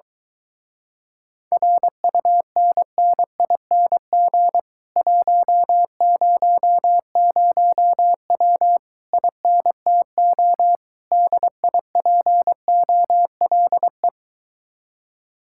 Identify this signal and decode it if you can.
Morse code, heard as a keyed tone, about 22 wpm, 700 Hz. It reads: RUNNING 100W INTO DIPOLE